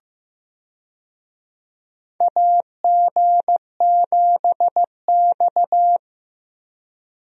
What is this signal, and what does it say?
Morse code, heard as a keyed tone, about 15 wpm, 700 Hz. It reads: AG7X
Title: USA Call Signs - 4 Characters - Repeated 3x - 15wpm